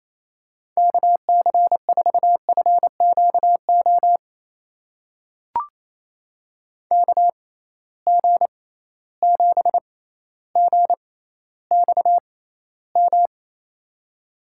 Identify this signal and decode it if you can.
Morse code, heard as a keyed tone, about 28 wpm, spaced as for 14 wpm, 700 Hz. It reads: KC4FQO KG7GXM
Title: USA Call Signs - 6 Characters - 6x Character Spacing - 28wpm